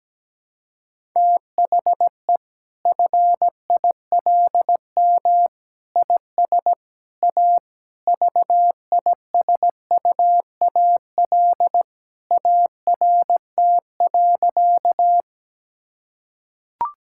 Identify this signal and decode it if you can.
Morse code, heard as a keyed tone, about 17 wpm, 700 Hz. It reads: THE FILM IS A VISUAL ART.
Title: Sentences from Top 1000 Words - Review Part 6 - 17wpm